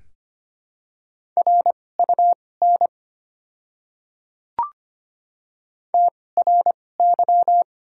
Morse code, heard as a keyed tone, about 25 wpm, spaced as for 18 wpm, 700 Hz. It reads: RUN TRY
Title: 2x Character Spacing - Top 300 Words - 25wpm